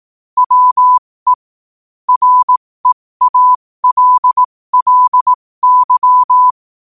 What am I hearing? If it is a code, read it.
Morse code, heard as a keyed tone, about 18 wpm, spaced as for 15 wpm, 1000 Hz. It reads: WE REALLY
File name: qotdfiles / 15wpm